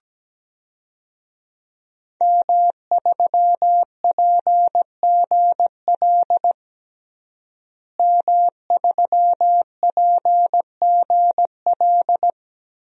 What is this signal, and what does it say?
Morse code, heard as a keyed tone, about 17 wpm, 700 Hz. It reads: M3PGL M3PGL
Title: DXCC-100 Call Signs - 5 Characters - Repeated 2x - 17wpm